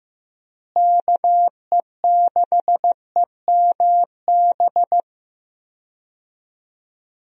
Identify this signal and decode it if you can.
Morse code, heard as a keyed tone, about 15 wpm, 700 Hz. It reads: KE6EMB
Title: USA Call Signs - 6 Characters - Repeated 2x - 15wpm